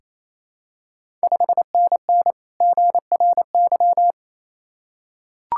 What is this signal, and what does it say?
Morse code, heard as a keyed tone, about 28 wpm, 700 Hz. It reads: 5NN GRY
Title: Ontario QSO Party Exchange 28wpm